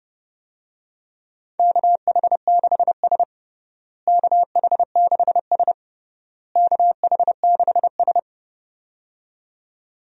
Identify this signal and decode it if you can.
Morse code, heard as a keyed tone, about 30 wpm, 700 Hz. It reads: KH6S KH6S KH6S
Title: DXCC-100 Call Signs - 4 Characters - Repeated 3x - 30wpm